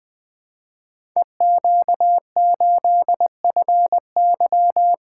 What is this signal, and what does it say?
Morse code, heard as a keyed tone, about 20 wpm, 700 Hz. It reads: EQ8FY
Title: Code Groups - 5 Letter:Numbers - 20wpm